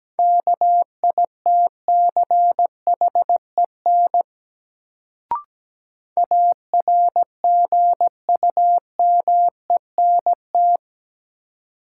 Morse code, heard as a keyed tone, about 17 wpm, 700 Hz. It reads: KITCHEN ARGUMENT
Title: Sentences from Top 1400 Words - New Word Drill - 17wpm